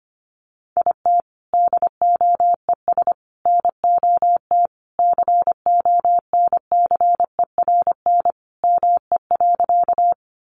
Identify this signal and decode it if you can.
Morse code, heard as a keyed tone, about 25 wpm, 700 Hz. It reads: IT DOES NOT CONCERN ME.
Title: Sentences from Top 1000 Words 25wpm